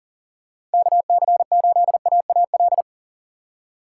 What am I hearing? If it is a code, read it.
Morse code, heard as a keyed tone, about 40 wpm, 700 Hz. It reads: KC8AAL
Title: USA Call Signs - 6 Characters - 3x Character Spacing - 40wpm